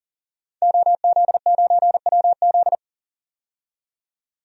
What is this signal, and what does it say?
Morse code, heard as a keyed tone, about 40 wpm, 700 Hz. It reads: OZ9WZ
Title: DXCC-100 Call Signs - 5 Characters - 40wpm